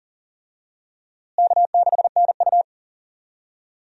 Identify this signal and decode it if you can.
Morse code, heard as a keyed tone, about 40 wpm, 700 Hz. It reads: K6NU
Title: USA Call Signs - 4 Characters - Repeated 2x - 40wpm